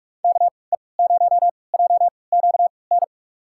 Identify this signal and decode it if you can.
Morse code, heard as a keyed tone, about 42 wpm, spaced as for 25 wpm, 700 Hz. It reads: KE0JQN
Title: USA Call Signs - 6 Characters - 3x Character Spacing - 45wpm